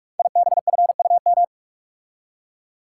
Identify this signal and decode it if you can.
Morse code, heard as a keyed tone, about 44 wpm, 700 Hz. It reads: IBFUM
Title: Code Groups - 5 Letter:Numbers - 45wpm